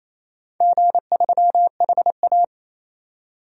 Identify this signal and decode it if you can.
Morse code, heard as a keyed tone, about 28 wpm, 700 Hz. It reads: G3HA
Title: DXCC-100 Call Signs - 4 Characters - 28wpm